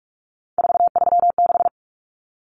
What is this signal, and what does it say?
Morse code, heard as a keyed tone, about 44 wpm, 700 Hz. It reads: 436
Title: Numbers - 3-Digit 45wpm